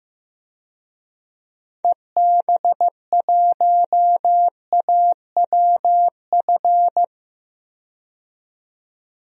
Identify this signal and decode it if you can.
Morse code, heard as a keyed tone, about 16 wpm, 700 Hz. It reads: EB1AWF
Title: DXCC-100 Call Signs - 6 Characters - 15wpm